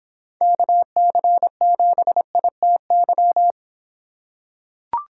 Morse code, heard as a keyed tone, about 26 wpm, 700 Hz. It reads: KC7ITY